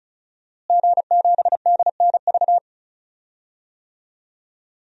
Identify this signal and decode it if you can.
Morse code, heard as a keyed tone, about 34 wpm, 700 Hz. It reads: G7DNV
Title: DXCC-100 Call Signs - 5 Characters - Repeated 3x - 35wpm